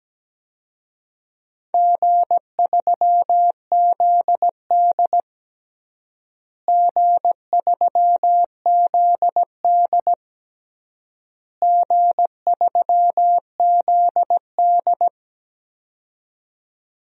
Morse code, heard as a keyed tone, about 17 wpm, 700 Hz. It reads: G3ZD G3ZD G3ZD
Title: DXCC-100 Call Signs - 4 Characters - Repeated 3x - 17wpm